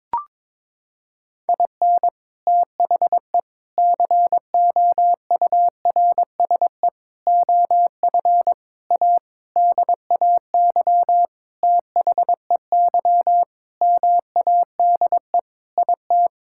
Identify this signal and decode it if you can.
Morse code, heard as a keyed tone, about 22 wpm, 700 Hz. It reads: IN THE COURSE OF A DAY THEY MADE IT
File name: practice / 22wpm / Sentences with 4-Word Phrases - 22wpm